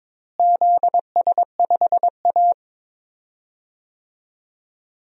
Morse code, heard as a keyed tone, about 22 wpm, 700 Hz. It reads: ZS5A
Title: DXCC-100 Call Signs - 4 Characters - Repeated 2x - 22wpm